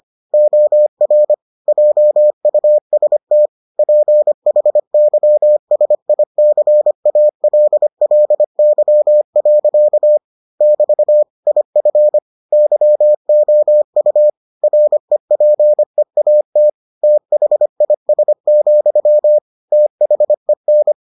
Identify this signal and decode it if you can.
Morse code, heard as a keyed tone, about 25 wpm, 600 Hz. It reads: OR JUST PHYSICALLY. <BT> IF YOU REPEAT THIS, THEN